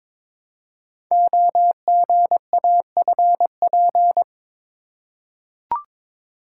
Morse code, heard as a keyed tone, about 22 wpm, 700 Hz. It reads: OGAFP
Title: Code Groups - 5 Letters - 22wpm